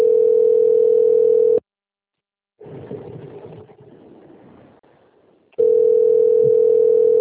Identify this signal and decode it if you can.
Ringback tone (North America), 440 + 480 Hz.